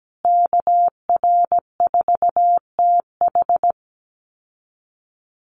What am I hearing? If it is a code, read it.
Morse code, heard as a keyed tone, about 17 wpm, 700 Hz. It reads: KR4TH
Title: Call Signs Repeated 3x - 17wpm